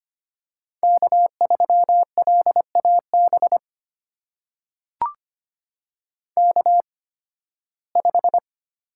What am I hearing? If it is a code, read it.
Morse code, heard as a keyed tone, about 25 wpm, spaced as for 16 wpm, 700 Hz. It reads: K3LAB K5
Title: USA Call Signs - 5 Characters - 8x Character Spacing - 25wpm